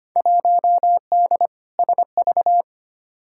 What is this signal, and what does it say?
Morse code, heard as a keyed tone, about 25 wpm, 700 Hz. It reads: 1D SV
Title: Field Day Exchanges - 25wpm